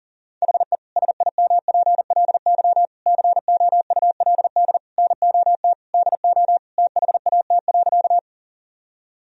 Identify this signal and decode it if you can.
Morse code, heard as a keyed tone, about 37 wpm, 700 Hz. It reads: HE SIMPLY COULD NOT DO THAT.
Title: Sentences from Top 1000 Words - Review Part 4 - 40wpm